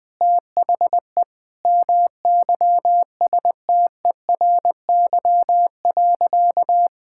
Morse code, heard as a keyed tone, about 20 wpm, 700 Hz. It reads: THE MYSTERY.
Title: Sentences with 7-Word Phrases - 20wpm